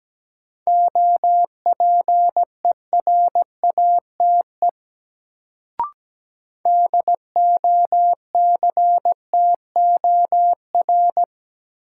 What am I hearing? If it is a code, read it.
Morse code, heard as a keyed tone, about 17 wpm, 700 Hz. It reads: OPERATE DOCTOR
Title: Sentences from Top 1500 Words - New Word Drill - 17wpm